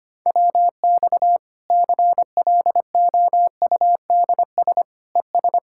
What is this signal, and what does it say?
Morse code, heard as a keyed tone, about 25 wpm, 700 Hz. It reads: WX CLOUDS ES
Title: Practice QSO - Exchange 2 - Info - 25wpm